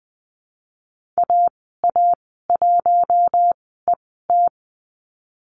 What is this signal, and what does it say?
Morse code, heard as a keyed tone, about 20 wpm, spaced as for 15 wpm, 700 Hz. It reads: AA1ET
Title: 069 Lesson 12g - Letter d - Call Signs 20wpm